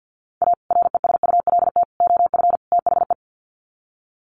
Morse code, heard as a keyed tone, about 48 wpm, 700 Hz. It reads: A RESULT OF THE
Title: Sets of 4 Words 50wpm